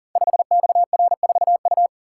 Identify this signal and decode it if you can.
Morse code, heard as a keyed tone, about 40 wpm, 700 Hz. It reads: 5XR4U
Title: Code Groups - 5 Letter:Numbers - 40wpm